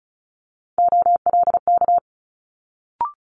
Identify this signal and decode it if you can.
Morse code, heard as a keyed tone, about 36 wpm, 700 Hz. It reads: OLK
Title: Multi-letter Phonemes Quick Fire 35wpm